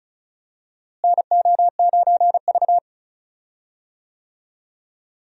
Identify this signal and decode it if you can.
Morse code, heard as a keyed tone, about 34 wpm, 700 Hz. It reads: NO9V
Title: USA Call Signs - 4 Characters - Repeated 2x - 35wpm